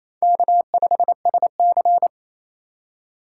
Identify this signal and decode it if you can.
Morse code, heard as a keyed tone, about 28 wpm, 700 Hz. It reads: K5SC
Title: USA Call Signs - 4 Characters - 3x Character Spacing - 28wpm